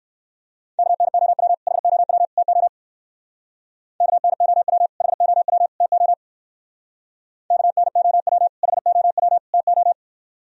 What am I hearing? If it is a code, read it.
Morse code, heard as a keyed tone, about 57 wpm, 700 Hz. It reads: KNOW HOW TO KNOW HOW TO KNOW HOW TO